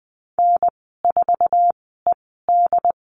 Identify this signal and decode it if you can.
Morse code, heard as a keyed tone, about 20 wpm, spaced as for 14 wpm, 700 Hz. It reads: N4ED
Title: 082 Lesson 14g - Number 2 - Call Signs 20wpm